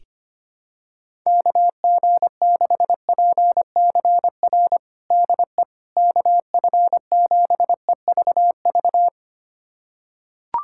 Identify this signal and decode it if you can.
Morse code, heard as a keyed tone, about 25 wpm, 700 Hz. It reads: KG6PCR DE KF7EVV
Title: USA Callsign DE Callsign - 6 Characters - 25wpm